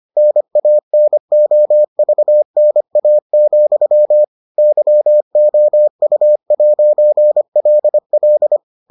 Morse code, heard as a keyed tone, about 25 wpm, 600 Hz. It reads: NANOVNA, YOU'LL